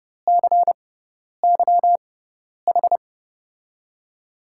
Morse code, heard as a keyed tone, about 28 wpm, spaced as for 10 wpm, 700 Hz. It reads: CYH
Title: USA Call Signs - 5 Characters - 6x Character Spacing - 30wpm